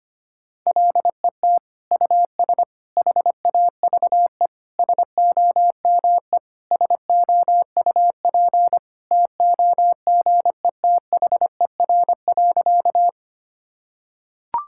Morse code, heard as a keyed tone, about 25 wpm, 700 Hz. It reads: LET US HAVE SOME SOUP TOGETHER.
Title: Sentences from Top 1000 Words - Review Part 6 - 25wpm